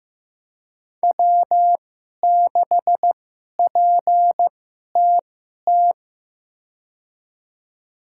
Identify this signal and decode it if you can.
Morse code, heard as a keyed tone, about 16 wpm, spaced as for 11 wpm, 700 Hz. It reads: W6PTT
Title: USA Call Signs - 5 Characters - 2x Character Spacing - 15wpm